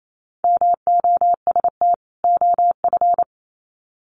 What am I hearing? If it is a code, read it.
Morse code, heard as a keyed tone, about 28 wpm, 700 Hz. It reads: MOST OF
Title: Sets of 2 Words 28wpm